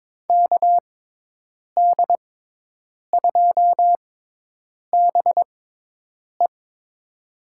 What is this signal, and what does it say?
Morse code, heard as a keyed tone, about 23 wpm, spaced as for 8 wpm, 700 Hz. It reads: KD2BE
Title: USA Call Signs - 5 Characters - 6x Character Spacing - 22wpm